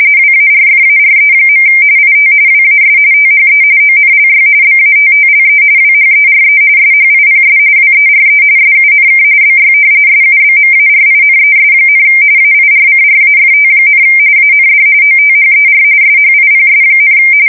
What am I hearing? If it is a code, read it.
RTTY, about 45.45 baud. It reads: WARFARE. IT WAS THE ONLY BATTLE WHERE SHIPS WERE SUNK BY SUBMARINES, AIRCRAFT, AND OTHER SHIPS,
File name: fsk170 / battleshipnj